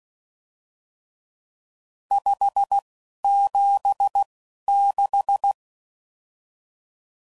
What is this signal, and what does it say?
Morse code, heard as a keyed tone, about 16 wpm, spaced as for 11 wpm, 800 Hz. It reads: 576
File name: CW Code Groups @10WPM / RSTs / Various RST No's